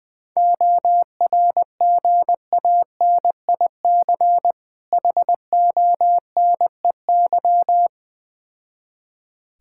Morse code, heard as a keyed tone, about 21 wpm, 700 Hz. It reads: ORGANIC HONEY
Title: Difficult Words with Hints - 20wpm